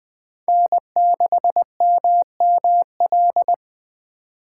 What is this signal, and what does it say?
Morse code, heard as a keyed tone, about 20 wpm, 700 Hz. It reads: N6MML
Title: USA Call Signs - 5 Characters - 4x Character Spacing - 20wpm